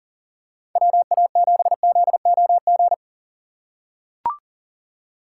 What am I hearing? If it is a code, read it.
Morse code, heard as a keyed tone, about 40 wpm, 700 Hz. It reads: WA7ZOG
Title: USA Call Signs - 6 Characters - 2x Character Spacing - 40wpm